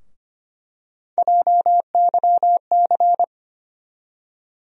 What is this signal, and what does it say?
Morse code, heard as a keyed tone, about 25 wpm, 700 Hz. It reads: JYC
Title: USA Call Sign Postfixes - 3 Letters 25wpm